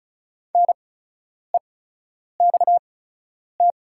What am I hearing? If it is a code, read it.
Morse code, heard as a keyed tone, about 36 wpm, spaced as for 10 wpm, 690 Hz. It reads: NEXT
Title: 8x Character Spacing - Top 500 Words - 35wpm